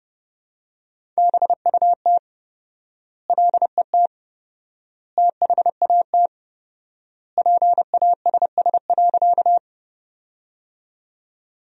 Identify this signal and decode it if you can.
Morse code, heard as a keyed tone, about 30 wpm, 700 Hz. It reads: BUT LET THAT PASS.
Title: 4x Word-Spacing - Sentences from Top 400 Words - 30wpm